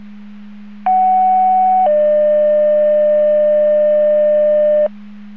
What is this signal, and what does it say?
Two-tone page, heard as a one-second tone, then a three-second tone, about 750 then 600 Hz, with noise. It is heard from 0.9 to 4.9 s.